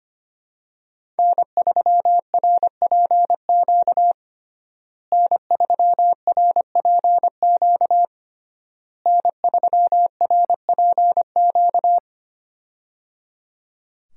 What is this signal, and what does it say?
Morse code, heard as a keyed tone, about 25 wpm, 700 Hz. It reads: N3RPQ N3RPQ N3RPQ
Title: USA Call Signs - 5 Characters - Repeated 3x - 25wpm